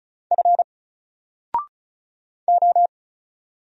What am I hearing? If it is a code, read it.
Morse code, heard as a keyed tone, about 36 wpm, 700 Hz. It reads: F O
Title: Single Letter-Number Alt 35wpm